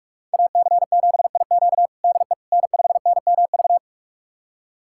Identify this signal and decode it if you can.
Morse code, heard as a keyed tone, about 42 wpm, 700 Hz. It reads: AC7IQ DE N5NMV
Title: USA Callsign DE Callsign - 5 Characters - 45wpm